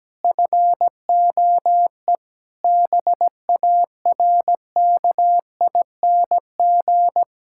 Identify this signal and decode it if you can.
Morse code, heard as a keyed tone, about 17 wpm, 700 Hz. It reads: FOE BARKING
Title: Pangram Challenge v2 - 17wpm